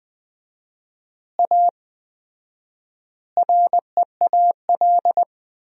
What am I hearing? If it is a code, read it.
Morse code, heard as a keyed tone, about 19 wpm, 700 Hz. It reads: A REAL
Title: 4x Word-Spacing - Sentences from Top 500 Words - 20wpm